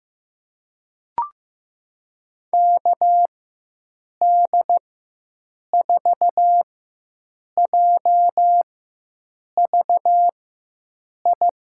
Morse code, heard as a keyed tone, about 15 wpm, spaced as for 7 wpm, 700 Hz. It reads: KD4JVI
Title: USA Call Signs - 6 Characters - 4x Character Spacing - 15wpm